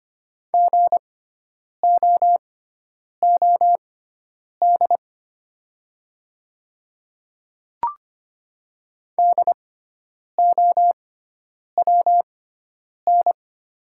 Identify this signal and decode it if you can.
Morse code, heard as a keyed tone, about 25 wpm, spaced as for 9 wpm, 690 Hz. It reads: GOOD DOWN
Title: 6x Character Spacing - Top 200 Words - 25wpm